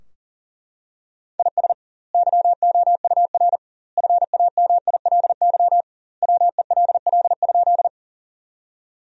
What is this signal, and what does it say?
Morse code, heard as a keyed tone, about 40 wpm, 700 Hz. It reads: IS YOUR FAMILY WELL?
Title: 2x Word-Spacing - Sentences from Top 400 Words - 40wpm